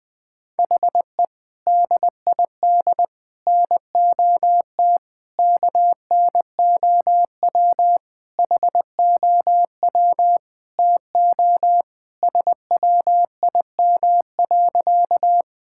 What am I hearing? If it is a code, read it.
Morse code, heard as a keyed tone, about 20 wpm, 700 Hz. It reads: HE DID NOT KNOW HOW TO SWIM.
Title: Sentences with 5-Word Phrases - 20wpm